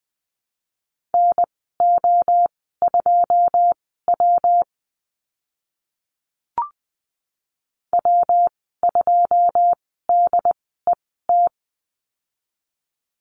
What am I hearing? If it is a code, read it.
Morse code, heard as a keyed tone, about 20 wpm, spaced as for 15 wpm, 700 Hz. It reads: NO2W W2DET
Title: 128 Lesson 21g - Number 6 - Call Signs 20wpm